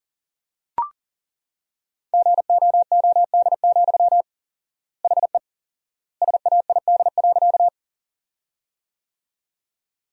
Morse code, heard as a keyed tone, about 40 wpm, 700 Hz. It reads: GOOD, HE SAID.
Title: 4x Word-Spacing - Sentences from Top 200 Words - 40wpm